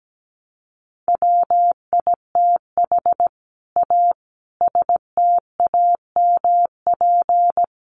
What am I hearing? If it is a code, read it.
Morse code, heard as a keyed tone, about 17 wpm, 700 Hz. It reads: WITH A STAMP
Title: A Sound Of Thunder 17wpm